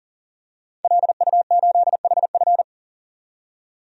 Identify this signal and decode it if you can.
Morse code, heard as a keyed tone, about 40 wpm, 700 Hz. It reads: LU8HF
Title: DXCC-100 Call Signs - 5 Characters - 40wpm